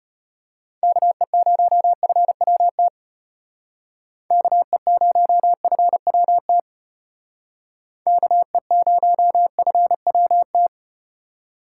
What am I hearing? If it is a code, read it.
Morse code, heard as a keyed tone, about 32 wpm, 700 Hz. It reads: KE0FWT KE0FWT KE0FWT